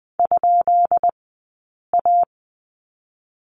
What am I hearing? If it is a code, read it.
Morse code, heard as a keyed tone, about 20 wpm, 700 Hz. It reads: ? A
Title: 137 Lesson 23d - Letter f - Two Characters 20wpm